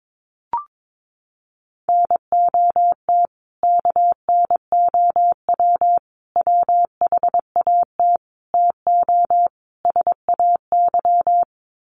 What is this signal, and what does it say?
Morse code, heard as a keyed tone, about 22 wpm, 700 Hz. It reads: NOT KNOW WHAT TO SAY
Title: Sets of 5 Words 22wpm